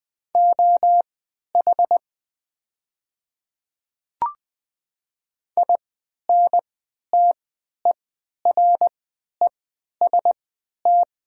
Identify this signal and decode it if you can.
Morse code, heard as a keyed tone, about 20 wpm, spaced as for 11 wpm, 700 Hz. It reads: OH INTEREST
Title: 3x Character Spacing - Top 500 Words - 20wpm